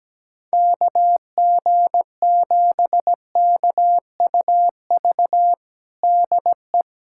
Morse code, heard as a keyed tone, about 17 wpm, 700 Hz. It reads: KG7KUV DE
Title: USA Callsign DE Callsign - 6 Characters - 17wpm